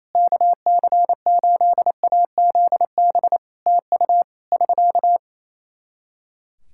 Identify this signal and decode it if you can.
Morse code, heard as a keyed tone, about 28 wpm, 700 Hz. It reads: KC8AZB TU <SK>
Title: Practice QSO - Exchange 4 - Ending - 28wpm